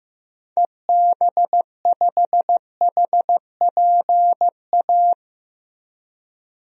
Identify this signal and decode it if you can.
Morse code, heard as a keyed tone, about 15 wpm, 700 Hz. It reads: EB5HPA
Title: DXCC-100 Call Signs - 6 Characters - Repeated 3x - 15wpm